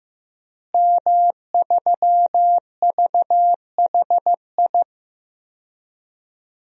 Morse code, heard as a keyed tone, about 16 wpm, 700 Hz. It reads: M3VHI
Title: DXCC-100 Call Signs - 5 Characters - Repeated 2x - 15wpm